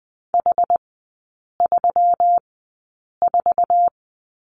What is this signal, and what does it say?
Morse code, heard as a keyed tone, about 20 wpm, spaced as for 8 wpm, 700 Hz. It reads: H34
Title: 121 Lesson 20e - Number 3 - Three Characters 20wpm